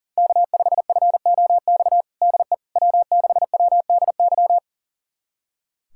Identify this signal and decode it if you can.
Morse code, heard as a keyed tone, about 37 wpm, 700 Hz. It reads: K5FOX DE W6WDY
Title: USA Callsign DE Callsign - 5 Characters - 40wpm